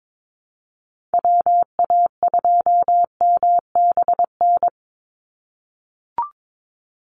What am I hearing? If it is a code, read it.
Morse code, heard as a keyed tone, about 22 wpm, 700 Hz. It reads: WA2MBN
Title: USA Call Signs 22wpm